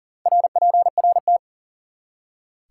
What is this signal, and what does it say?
Morse code, heard as a keyed tone, about 51 wpm, 700 Hz. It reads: RPRT